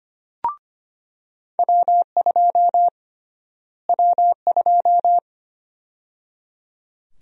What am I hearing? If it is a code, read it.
Morse code, heard as a keyed tone, about 25 wpm, 700 Hz. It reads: W2 W2
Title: USA Call Sign Prefixes - Repeated 2x - 25wpm